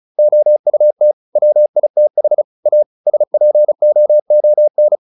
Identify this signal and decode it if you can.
Morse code, heard as a keyed tone, about 34 wpm, 600 Hz. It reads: OUT WITH A SPOON
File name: the old man and the sea 2 35wpm